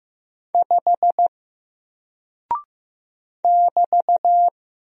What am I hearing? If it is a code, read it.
Morse code, heard as a keyed tone, about 15 wpm, 700 Hz. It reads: 5 <BT>
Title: Single Character (Letter:Number:Punctuation:Prosign) Alt 15wpm